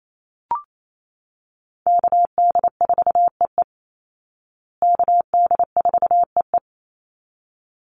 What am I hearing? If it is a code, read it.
Morse code, heard as a keyed tone, about 28 wpm, 700 Hz. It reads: KD4EE KD4EE
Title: Call Signs Repeated 3x - 28wpm